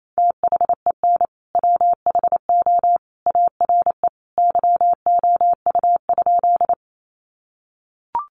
Morse code, heard as a keyed tone, about 28 wpm, 700 Hz. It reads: THEN WHO ARE YOU?
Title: Sentences from Top 100 Words 28wpm